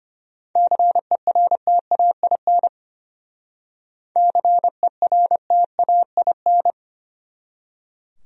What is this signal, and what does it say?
Morse code, heard as a keyed tone, about 32 wpm, 700 Hz. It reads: CERTAIN CERTAIN